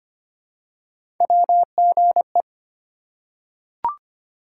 Morse code, heard as a keyed tone, about 25 wpm, 700 Hz. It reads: WGE
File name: Three Letter-Number Combinations - 25wpm